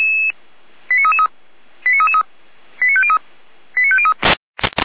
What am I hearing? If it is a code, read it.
Five-tone selective call (ZVEI), heard as quick runs of five tones, tones of about 70 ms. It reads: 89333 89333 89503 89503